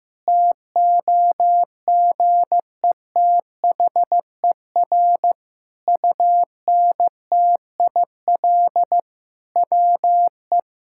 Morse code, heard as a keyed tone, about 15 wpm, 700 Hz. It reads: TOGETHER UNTIL WE
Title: Sentences from Top 1600 Words 15wpm